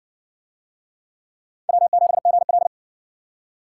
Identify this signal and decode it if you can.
Morse code, heard as a keyed tone, about 60 wpm, 700 Hz. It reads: W7GL